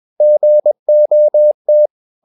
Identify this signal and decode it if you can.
Morse code, heard as a keyed tone, about 21 wpm, 600 Hz. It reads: GOT
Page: CW med Gnister